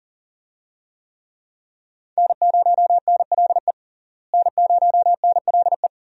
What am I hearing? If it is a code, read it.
Morse code, heard as a keyed tone, about 40 wpm, 700 Hz. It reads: N0NLE N0NLE
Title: USA Call Signs - 5 Characters - Repeated 3x - 40wpm